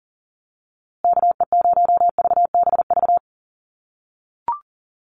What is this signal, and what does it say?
Morse code, heard as a keyed tone, about 40 wpm, 700 Hz. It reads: KE0VBV
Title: USA Call Signs Repeated 3x 40wpm